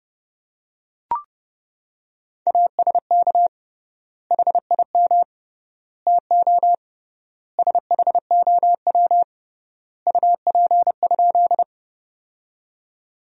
Morse code, heard as a keyed tone, about 30 wpm, 700 Hz. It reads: ASK HIM TO SHOW UP?
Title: 3x Word-Spacing - Sentences from Top 200 Words - 30wpm